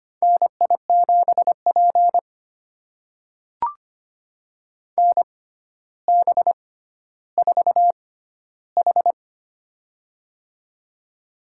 Morse code, heard as a keyed tone, about 25 wpm, spaced as for 13 wpm, 700 Hz. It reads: NI7P NB4H
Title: USA Call Signs - 4 Characters - 6x Character Spacing - 25wpm